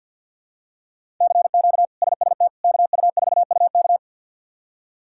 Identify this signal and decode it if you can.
Morse code, heard as a keyed tone, about 53 wpm, 700 Hz. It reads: CQ SST KU4UK
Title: K1USN SST QSO Practice - 50wpm